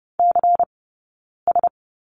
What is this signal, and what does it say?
Morse code, heard as a keyed tone, about 30 wpm, 700 Hz. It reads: C S
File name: Lesson 20d - Number 3 - Two Characters 30wpm